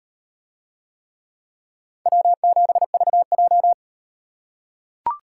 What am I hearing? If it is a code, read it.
Morse code, heard as a keyed tone, about 35 wpm, 700 Hz. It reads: W7VJ